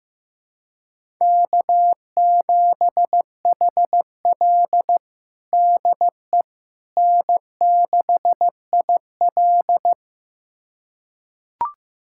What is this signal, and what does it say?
Morse code, heard as a keyed tone, about 15 wpm, 700 Hz. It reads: K7HL DE N6IL
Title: USA Callsign DE Callsign - 4 Characters - 15wpm